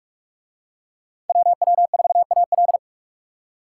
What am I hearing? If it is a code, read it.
Morse code, heard as a keyed tone, about 40 wpm, 700 Hz. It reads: WW4AL